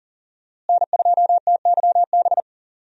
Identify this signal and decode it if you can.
Morse code, heard as a keyed tone, about 40 wpm, 700 Hz. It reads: N2TYB
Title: USA Call Signs - 5 Characters - Repeated 3x - 40wpm